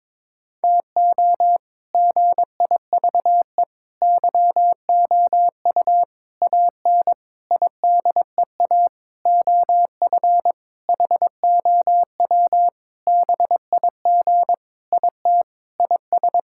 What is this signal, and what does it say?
Morse code, heard as a keyed tone, about 22 wpm, 700 Hz. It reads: TO GIVE YOU AN IDEA OF HOW BIG IT IS
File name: Sentences with 7-Word Phrases - 22wpm